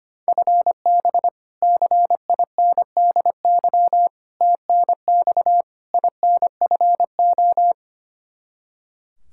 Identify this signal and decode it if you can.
Morse code, heard as a keyed tone, about 25 wpm, 700 Hz. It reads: FB CINDY TNX INFO
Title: Practice QSO - Exchange 3 - More Info - 25wpm